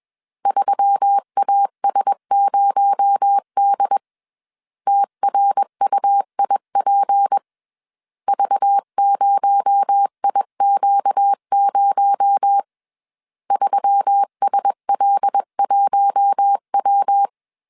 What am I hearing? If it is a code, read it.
Morse code, heard as a keyed tone, about 21 wpm, 800 Hz. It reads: 3AS0D TRUIP V0IQ0 3SL1W
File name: CW / DET / 1 6 1500C S C 20wpm 2116